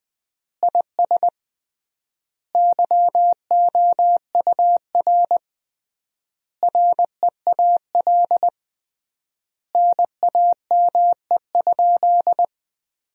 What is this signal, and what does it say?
Morse code, heard as a keyed tone, about 19 wpm, 700 Hz. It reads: IS YOUR REAL NAME?
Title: 3x Word-Spacing - Sentences from Top 300 Words - 20wpm